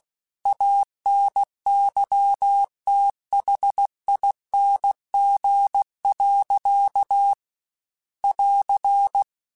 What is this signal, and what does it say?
Morse code, heard as a keyed tone, about 16 wpm, 800 Hz. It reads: ANYTHING. <AR>
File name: Quotes / Quotes-mp3s / StandFor @15